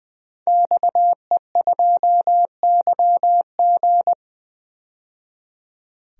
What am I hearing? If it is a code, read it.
Morse code, heard as a keyed tone, about 20 wpm, 700 Hz. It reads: XE2YG
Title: DXCC-100 Call Signs - 5 Characters - 20wpm